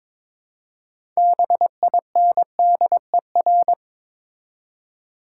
Morse code, heard as a keyed tone, about 23 wpm, 690 Hz. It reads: BINDER
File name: Difficult Words with Hints - 22wpm